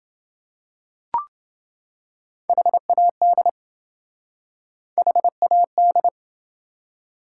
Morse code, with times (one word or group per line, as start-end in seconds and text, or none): 2.5-3.5 HAD
5.0-6.1 HAD